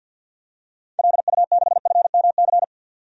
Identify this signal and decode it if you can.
Morse code, heard as a keyed tone, about 53 wpm, 700 Hz. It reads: LU6FMC
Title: DXCC-100 Call Signs - 6 Characters - 50wpm